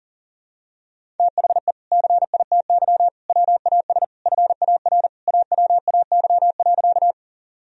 Morse code, heard as a keyed tone, about 37 wpm, 700 Hz. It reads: THE CITY WAS FAR AWAY.
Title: Sentences from Top 1000 Words - Review Part 1 - 40wpm